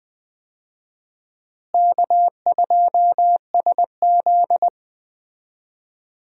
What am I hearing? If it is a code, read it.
Morse code, heard as a keyed tone, about 19 wpm, 700 Hz. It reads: K2SZ
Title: USA Call Signs - 4 Characters - Repeated 2x - 20wpm